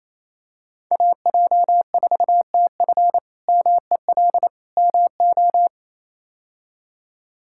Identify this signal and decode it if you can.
Morse code, heard as a keyed tone, about 28 wpm, 700 Hz. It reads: AJ4TF MEL MO
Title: K1USN SST QSO Practice - 28wpm